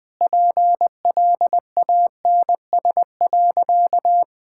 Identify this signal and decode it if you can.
Morse code, heard as a keyed tone, about 20 wpm, 700 Hz. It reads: PLANS.
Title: Sentences with 5-Word Phrases - 20wpm